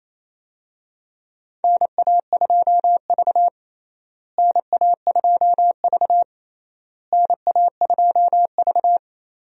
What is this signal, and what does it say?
Morse code, heard as a keyed tone, about 27 wpm, 700 Hz. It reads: NA2V NA2V NA2V
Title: USA Call Signs - 4 Characters - Repeated 3x - 28wpm